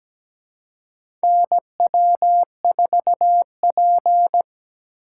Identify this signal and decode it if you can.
Morse code, heard as a keyed tone, about 17 wpm, 700 Hz. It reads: NW4P
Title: USA Call Signs - 4 Characters - 4x Character Spacing - 17wpm